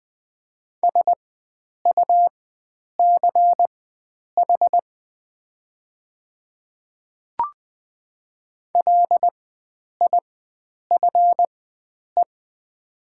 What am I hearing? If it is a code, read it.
Morse code, heard as a keyed tone, about 20 wpm, spaced as for 9 wpm, 700 Hz. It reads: SUCH LIFE
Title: 4x Character Spacing - Top 300 Words - 20wpm